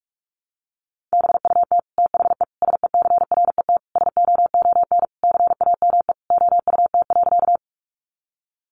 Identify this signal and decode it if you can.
Morse code, heard as a keyed tone, about 44 wpm, 700 Hz. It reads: BUT THE SECRET SOON CAME OUT.
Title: Sentences from Top 1200 Words 45wpm